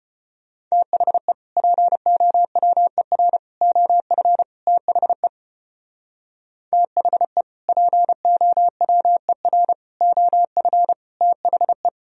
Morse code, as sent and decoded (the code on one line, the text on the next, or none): - .... / .--. --- .-- . .-. / --- ..-. / - .... / - .... / .--. --- .-- . .-. / --- ..-. / - .... .
THE POWER OF THE THE POWER OF THE